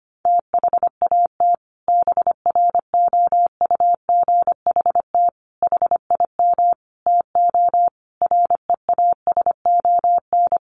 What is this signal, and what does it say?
Morse code, heard as a keyed tone, about 25 wpm, 700 Hz. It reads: THAT BROUGHT HIM TO REASON
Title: Sentences from Top 700 Words 25wpm